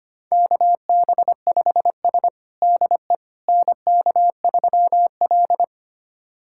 Morse code, heard as a keyed tone, about 25 wpm, 700 Hz. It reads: KB5S DE NK3L
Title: USA Callsign DE Callsign - 4 Characters - 25wpm